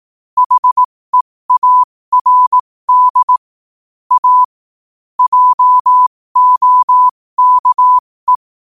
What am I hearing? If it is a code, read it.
Morse code, heard as a keyed tone, about 18 wpm, spaced as for 15 wpm, 1000 Hz. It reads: HEARD A JOKE
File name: qotdfiles / 15wpm